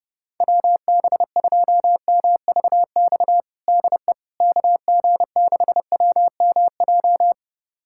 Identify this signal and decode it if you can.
Morse code, heard as a keyed tone, about 29 wpm, 700 Hz. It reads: WB2MVX DE KG6WMJ
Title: USA Callsign DE Callsign - 6 Characters - 30wpm